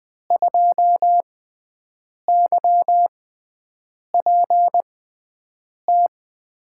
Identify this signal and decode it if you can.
Morse code, heard as a keyed tone, about 20 wpm, 700 Hz. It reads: 2 Y P T
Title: USA Call Signs - 6x Character Spacing - 20wpm